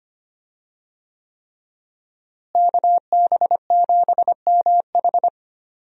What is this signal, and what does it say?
Morse code, heard as a keyed tone, about 25 wpm, 700 Hz. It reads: KB7MH
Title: USA Call Signs - 5 Characters - Repeated 3x - 25wpm